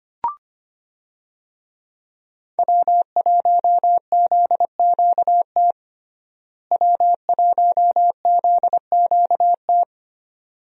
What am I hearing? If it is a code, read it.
Morse code, heard as a keyed tone, about 25 wpm, 700 Hz. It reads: W1ZQT W1ZQT
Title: USA Call Signs - 5 Characters - Repeated 2x - 25wpm